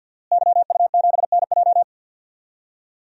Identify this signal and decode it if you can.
Morse code, heard as a keyed tone, about 51 wpm, 700 Hz. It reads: YU7NJ
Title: DXCC-100 Call Signs - 5 Characters - Repeated 2x - 50wpm